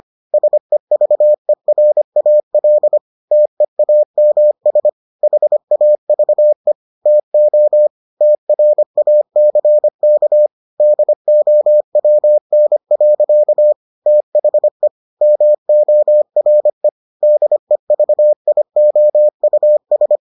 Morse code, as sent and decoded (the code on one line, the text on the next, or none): ... ...- . .-. .- .-.. / - . .- -- ... / .... .- ...- . / - --- / - .-. .- -.-. -.- / -.. --- .-- -. .-.-.- / - .... / -- --- .-. / -.. ...- .. --- ..- ...
SEVERAL TEAMS HAVE TO TRACK DOWN. THE MORE DEVIOUS